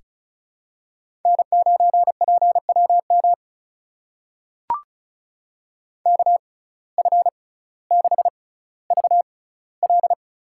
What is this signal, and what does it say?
Morse code, heard as a keyed tone, about 34 wpm, spaced as for 18 wpm, 700 Hz. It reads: N9PWM KF6VL
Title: USA Call Signs - 6x Character Spacing - 35wpm